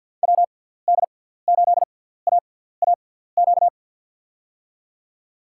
Morse code, heard as a keyed tone, about 53 wpm, spaced as for 17 wpm, 700 Hz. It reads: WD8AAQ
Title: USA Call Signs - 6 Characters - 6x Character Spacing - 50wpm